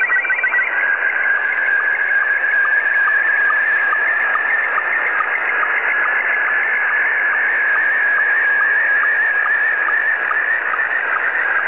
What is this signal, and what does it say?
SSTV (slow-scan television), Scottie 1.
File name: legacy / soundsofhf / ama-sstv